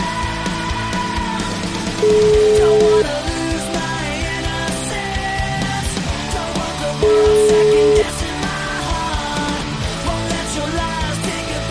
Ringback tone (Europe, 425 Hz), with noise.